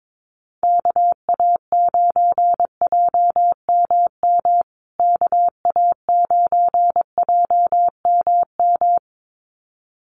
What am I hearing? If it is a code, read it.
Morse code, heard as a keyed tone, about 22 wpm, 700 Hz. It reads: KA9JMM KA9JMM
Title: USA Call Signs Repeated 2x 22wpm